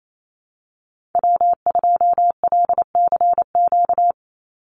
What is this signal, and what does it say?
Morse code, heard as a keyed tone, about 28 wpm, 700 Hz. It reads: W2LCQ
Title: Call Signs Repeated 2x - 28wpm